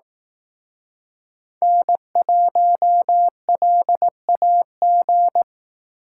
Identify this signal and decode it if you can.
Morse code, heard as a keyed tone, about 18 wpm, 700 Hz. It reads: N1LAG
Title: Speed Racing - USA 1x3 Call Signs 15wpm